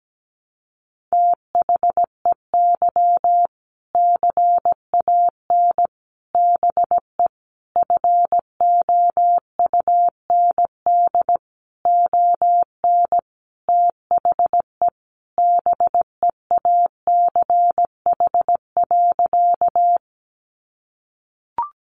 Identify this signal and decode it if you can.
Morse code, heard as a keyed tone, about 17 wpm, 700 Hz. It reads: THEY CAN BE FOUND ON THE BEACH.
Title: Sentences from Top 1500 Words 17wpm